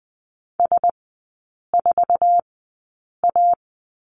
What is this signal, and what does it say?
Morse code, heard as a keyed tone, about 20 wpm, spaced as for 8 wpm, 700 Hz. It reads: S4A
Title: 047 Lesson 9e - Number 4 - Three Characters 20wpm